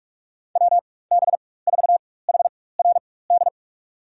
Morse code, heard as a keyed tone, about 47 wpm, spaced as for 21 wpm, 700 Hz. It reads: WB4HRD
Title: USA Call Signs - 6 Characters - 4x Character Spacing - 45wpm